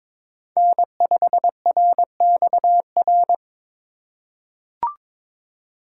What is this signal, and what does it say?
Morse code, heard as a keyed tone, about 22 wpm, 700 Hz. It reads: N5RXR
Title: USA Call Signs - 4x Character Spacing - 22wpm